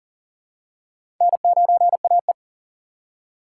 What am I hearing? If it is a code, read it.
Morse code, heard as a keyed tone, about 40 wpm, 700 Hz. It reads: N9AE
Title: USA Call Signs - 4 Characters - 3x Character Spacing - 40wpm